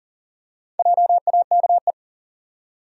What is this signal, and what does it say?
Morse code, heard as a keyed tone, about 39 wpm, 700 Hz. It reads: JAKE
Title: USA Common Names - 40wpm